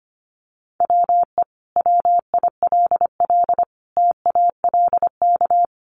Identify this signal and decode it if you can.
Morse code, heard as a keyed tone, about 25 wpm, 700 Hz. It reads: WE WILL TALK
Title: Sentences from Top 400 Words 25wpm